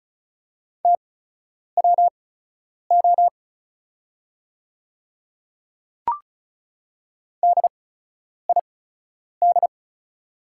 Morse code, heard as a keyed tone, about 36 wpm, spaced as for 10 wpm, 690 Hz. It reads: TWO DID
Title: 8x Character Spacing - Top 100 Words - 35wpm